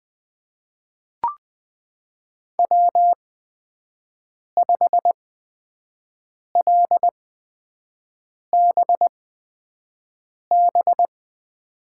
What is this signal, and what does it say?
Morse code, heard as a keyed tone, about 20 wpm, spaced as for 5 wpm, 700 Hz. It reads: W5LBB
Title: USA Call Signs - 5 Characters - 8x Character Spacing - 20wpm